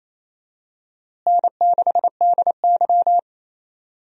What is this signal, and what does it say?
Morse code, heard as a keyed tone, about 28 wpm, 700 Hz. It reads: N6DY
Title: USA Call Signs - 4 Characters - Repeated 3x - 28wpm